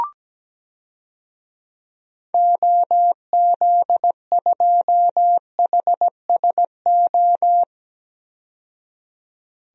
Morse code, heard as a keyed tone, about 17 wpm, 700 Hz. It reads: OZ2HSO
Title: DXCC-100 Call Signs - 6 Characters - 17wpm